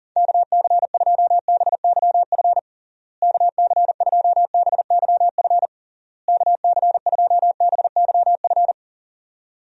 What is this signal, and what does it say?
Morse code, heard as a keyed tone, about 40 wpm, 700 Hz. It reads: KC2BYF KC2BYF KC2BYF
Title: USA Call Signs - 6 Characters - Repeated 3x - 40wpm